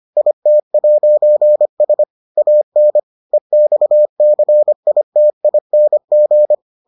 Morse code, heard as a keyed tone, about 25 wpm, 600 Hz. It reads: IT'S AN EXCITING